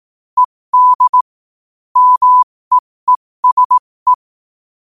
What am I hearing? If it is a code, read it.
Morse code, heard as a keyed tone, about 18 wpm, spaced as for 15 wpm, 1000 Hz. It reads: ED MEESE